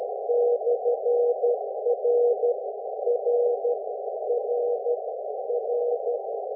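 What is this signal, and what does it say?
Morse code, heard as a keyed tone, about 12 wpm, 480 Hz, with noise. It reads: /R R R R
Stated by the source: QRP contest signal on 14 MHz at 1.5 W, sent with a KX1 transceiver, 7 m wire antenna and an R-143 key, portable.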